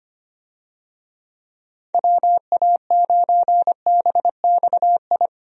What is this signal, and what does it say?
Morse code, heard as a keyed tone, about 25 wpm, 700 Hz. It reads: WA9BXI
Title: USA Call Signs - 6 Characters - 25wpm